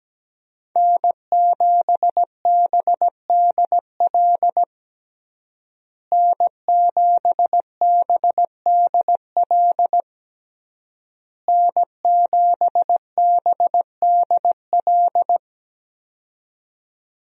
Morse code, heard as a keyed tone, about 17 wpm, 700 Hz. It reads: N7BDL N7BDL N7BDL
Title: USA Call Signs - 5 Characters - Repeated 3x - 17wpm